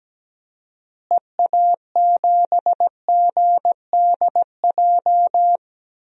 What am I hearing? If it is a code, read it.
Morse code, heard as a keyed tone, about 17 wpm, 700 Hz. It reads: EA7GDJ
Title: DXCC-100 Call Signs - 6 Characters - Repeated 3x - 17wpm